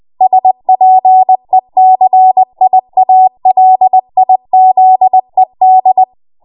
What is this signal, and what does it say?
Morse code, heard as a keyed tone, about 20 wpm, 750 Hz. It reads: SPECIALIZED
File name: Morse / 20WPM